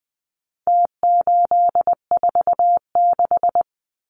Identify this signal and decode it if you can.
Morse code, heard as a keyed tone, about 20 wpm, 700 Hz. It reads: T846
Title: Numbers - Contest Style - 4-Digit 20wpm